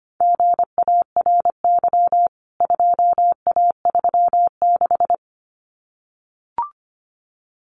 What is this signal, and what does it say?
Morse code, heard as a keyed tone, about 25 wpm, 700 Hz. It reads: GARY 2A36
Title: CWT Exchange 25wpm